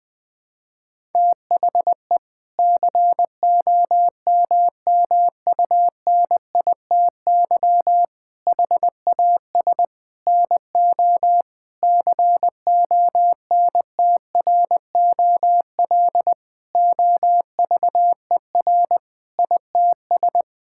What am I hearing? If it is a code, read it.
Morse code, heard as a keyed tone, about 20 wpm, 700 Hz. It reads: THE COMMUNITY HAS NO CONTROL OVER ITS
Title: Sentences from Top 1000 Words - Review Part 5 - 20wpm